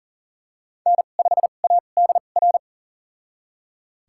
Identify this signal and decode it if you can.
Morse code, heard as a keyed tone, about 40 wpm, spaced as for 29 wpm, 700 Hz. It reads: N5ADR
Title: USA Call Signs - 5 Characters - 2x Character Spacing - 40wpm